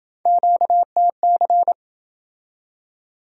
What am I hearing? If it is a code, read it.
Morse code, heard as a keyed tone, about 26 wpm, 700 Hz. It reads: QTC